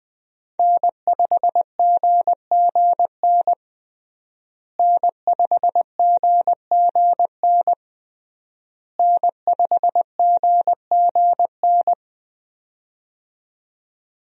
Morse code, heard as a keyed tone, about 20 wpm, 700 Hz. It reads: N5GGN N5GGN N5GGN
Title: USA Call Signs - 5 Characters - Repeated 3x - 20wpm